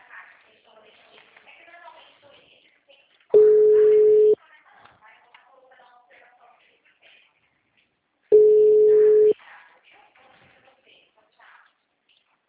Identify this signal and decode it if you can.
Ringback tone (Europe, 425 Hz), with speech.
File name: out- -2008- - - .59983